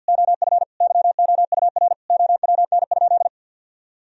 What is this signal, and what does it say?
Morse code, heard as a keyed tone, about 50 wpm, 700 Hz. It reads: OF YOUR OWN?
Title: Sentences from Top 1000 Words - Review Part 4 - 50wpm